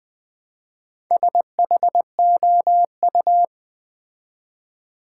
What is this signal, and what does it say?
Morse code, heard as a keyed tone, about 21 wpm, 700 Hz. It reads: SHOU
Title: Build A Word - 2 Letters At A Time - 6-Letter Words - 20wpm